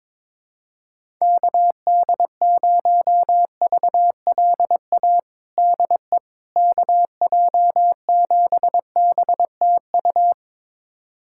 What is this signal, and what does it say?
Morse code, heard as a keyed tone, about 21 wpm, 700 Hz. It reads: KD0VLA DE KJ7BTU
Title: USA Callsign DE Callsign - 6 Characters - 22wpm